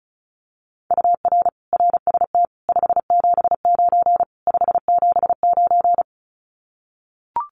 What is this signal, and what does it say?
Morse code, heard as a keyed tone, about 34 wpm, 700 Hz. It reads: UR RST 579 579
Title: Common QSO Phrases Encore 35wpm